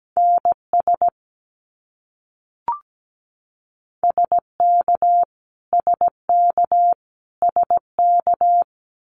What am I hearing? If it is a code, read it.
Morse code, heard as a keyed tone, about 17 wpm, 700 Hz. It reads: NS SK SK SK
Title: Canadian Provinces and Territories - Repeated 3x - 17wpm